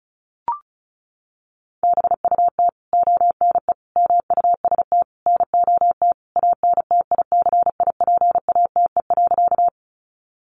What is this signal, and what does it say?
Morse code, heard as a keyed tone, about 34 wpm, 700 Hz. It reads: BUT ONE MUST NOT ANTICIPATE.
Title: Sentences from Top 1500 Words 35wpm